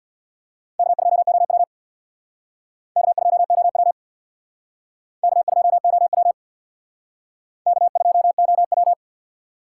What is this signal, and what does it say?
Morse code, heard as a keyed tone, about 61 wpm, 700 Hz. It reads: K2OW K2OW K2OW K2OW
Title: Speed Racing - USA 1x2 Call Signs 50wpm